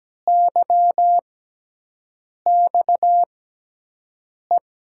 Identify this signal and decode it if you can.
Morse code, heard as a keyed tone, about 17 wpm, spaced as for 6 wpm, 700 Hz. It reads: YXE
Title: USA Call Signs - 6x Character Spacing - 17wpm